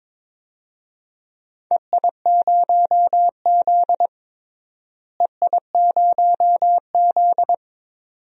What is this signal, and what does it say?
Morse code, heard as a keyed tone, about 21 wpm, 700 Hz. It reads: EI0Z EI0Z
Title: DXCC-100 Call Signs - 4 Characters - Repeated 2x - 22wpm